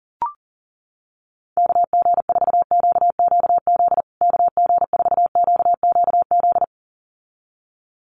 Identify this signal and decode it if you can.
Morse code, heard as a keyed tone, about 37 wpm, 700 Hz. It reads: KG4QQZ KG4QQZ
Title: USA Call Signs Repeated 2x 40wpm